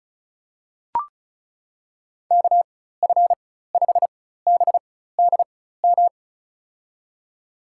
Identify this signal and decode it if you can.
Morse code, heard as a keyed tone, about 36 wpm, spaced as for 16 wpm, 700 Hz. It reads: KF5BDM
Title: USA Call Signs - 6 Characters - 4x Character Spacing - 35wpm